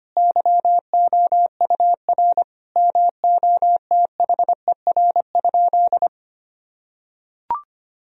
Morse code, heard as a keyed tone, about 25 wpm, 700 Hz. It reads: YOUR MOTHER?
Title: Sentences from Top 1000 Words - Review Part 3 - 25wpm